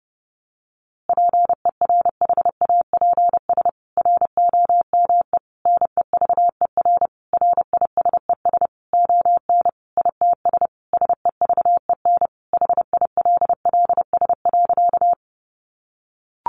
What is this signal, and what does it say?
Morse code, heard as a keyed tone, about 30 wpm, 700 Hz. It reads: PERHAPS ROME NEVER RISES ON ITS SEVEN HILLS.
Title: A Sound Of Thunder 30wpm